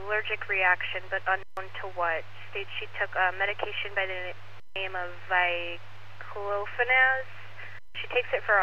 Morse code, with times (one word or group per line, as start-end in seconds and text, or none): none